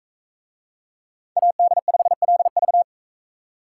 Morse code, heard as a keyed tone, about 44 wpm, 700 Hz. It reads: AD5LV